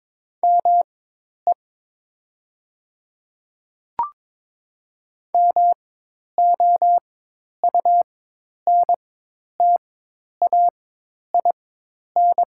Morse code, heard as a keyed tone, about 22 wpm, spaced as for 10 wpm, 700 Hz. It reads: ME MOUNTAIN
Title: 4x Character Spacing - Top 500 Words - 22wpm